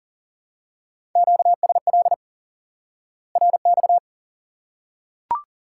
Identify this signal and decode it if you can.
Morse code, heard as a keyed tone, about 40 wpm, 700 Hz. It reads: QSL RX
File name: IQE Course 3 - Lesson 89 - YAGI - Two Elements - 40wpm